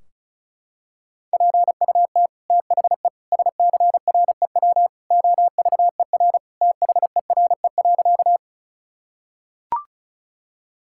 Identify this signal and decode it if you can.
Morse code, heard as a keyed tone, about 36 wpm, 700 Hz. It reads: PUT THE SCREW OVER THERE.
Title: Sentences from Top 1600 Words 35wpm